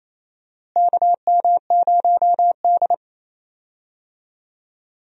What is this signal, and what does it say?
Morse code, heard as a keyed tone, about 28 wpm, 700 Hz. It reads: KM0D
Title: USA Call Signs - 4 Characters - Repeated 3x - 28wpm